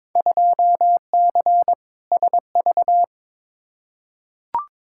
Morse code, heard as a keyed tone, about 22 wpm, 700 Hz. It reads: 2C SV
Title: Field Day Exchanges Encore - 22wpm